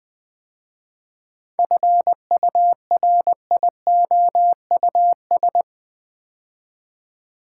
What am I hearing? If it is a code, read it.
Morse code, heard as a keyed tone, about 20 wpm, 700 Hz. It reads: FURIOUS
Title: 7 Letter Words - First 3 Letters Given - 20wpm